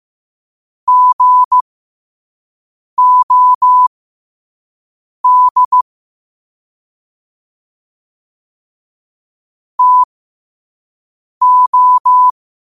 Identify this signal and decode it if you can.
Morse code, heard as a keyed tone, about 15 wpm, spaced as for 5 wpm, 1000 Hz. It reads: GOD TO
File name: qotdfiles / 5wpm